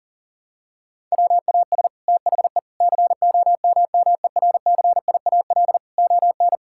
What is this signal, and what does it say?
Morse code, heard as a keyed tone, about 37 wpm, 700 Hz. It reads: WAS THE COMMERCIAL ON
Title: Sentences from Top 1000 Words - Review Part 5 - 40wpm